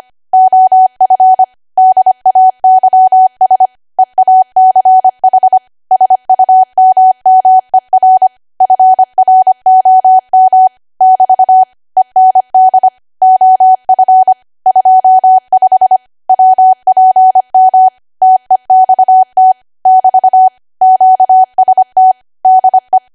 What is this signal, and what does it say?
Morse code, heard as a keyed tone, about 25 wpm, 750 Hz. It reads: OF DAYS EACH SUMMER FROM <BT> END OF 25 WPM TEXT <BT> QST DE